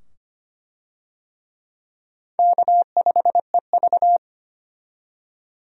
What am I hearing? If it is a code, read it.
Morse code, heard as a keyed tone, about 25 wpm, 700 Hz. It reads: K5EV
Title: USA Call Signs - 4 Characters - 25wpm